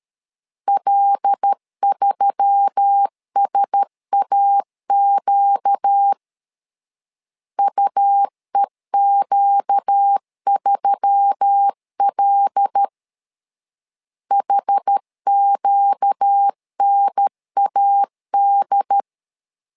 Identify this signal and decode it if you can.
Morse code, heard as a keyed tone, about 13 wpm, 800 Hz. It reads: L3SAQ UEQ3L HQNAD
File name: LEZIONI / 5 LBVC / 5 L 12wpm 1913